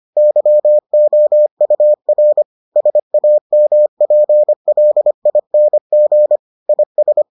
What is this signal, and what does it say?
Morse code, heard as a keyed tone, about 25 wpm, 600 Hz. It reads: YOUR SAMPLING IS